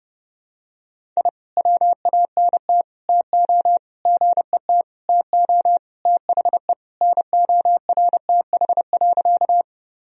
Morse code, heard as a keyed tone, about 30 wpm, 700 Hz. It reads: I WANT TO GET TO THE NORTH.
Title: Sentences from Top 1000 Words - Review Part 1 - 30wpm